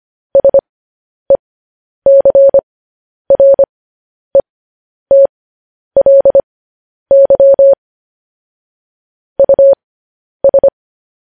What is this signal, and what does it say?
Morse code, heard as a keyed tone, about 25 wpm, spaced as for 10 wpm, 550 Hz. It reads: SECRETLY US
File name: podcast / News headlines-1x-25x10